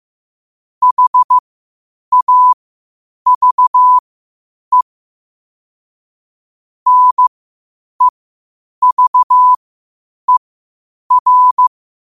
Morse code, heard as a keyed tone, about 15 wpm, spaced as for 8 wpm, 1000 Hz. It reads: HAVE NEVER
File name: qotdfiles / 8wpm